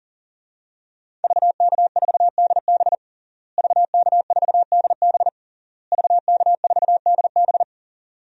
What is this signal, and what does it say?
Morse code, heard as a keyed tone, about 40 wpm, 700 Hz. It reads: VK4DB VK4DB VK4DB
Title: DXCC-100 Call Signs - 5 Characters - Repeated 3x - 40wpm